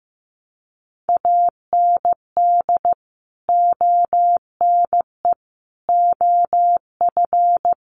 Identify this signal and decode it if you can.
Morse code, heard as a keyed tone, about 15 wpm, 700 Hz. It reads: AND ONE OF
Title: Sets of 5 Words 15wpm